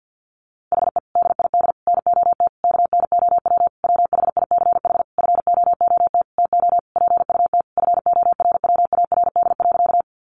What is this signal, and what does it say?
Morse code, heard as a keyed tone, about 50 wpm, 700 Hz. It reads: HE DID NOT KNOW WHICH FOOT TO PUT FORWARD.
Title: Sentences from Top 1300 Words 50wpm